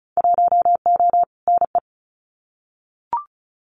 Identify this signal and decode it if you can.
Morse code, heard as a keyed tone, about 36 wpm, 700 Hz. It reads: 1O NE
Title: Winter Field Day Exchanges 35wpm